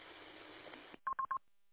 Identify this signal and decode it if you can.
There is no code.